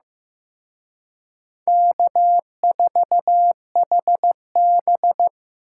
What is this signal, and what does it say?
Morse code, heard as a keyed tone, about 18 wpm, 700 Hz. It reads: K4HB
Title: Speed Racing - USA Call Signs 15wpm